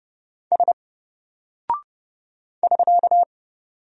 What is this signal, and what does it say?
Morse code, heard as a keyed tone, about 30 wpm, 700 Hz. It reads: S <SK>
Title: Single Character (Letter:Number:Punctuation:Prosign) 30wpm